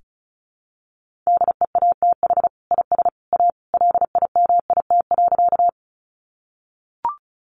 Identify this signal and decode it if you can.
Morse code, heard as a keyed tone, about 34 wpm, 700 Hz. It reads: DEATH IS A LIMIT.
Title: Sentences from Top 1500 Words 35wpm